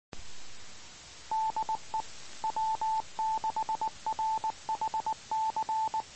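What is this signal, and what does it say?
Morse code, heard as a keyed tone, about 19 wpm, 880 Hz. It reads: DE W6RHC
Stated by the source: Morse code heard on 2 m at 146.850 MHz.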